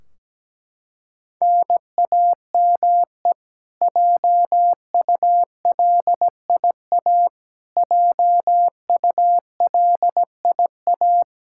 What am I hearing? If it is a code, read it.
Morse code, heard as a keyed tone, about 17 wpm, 700 Hz. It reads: NAME JULIA JULIA
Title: Practice QSO - Exchange 1 - Beginning - 17wpm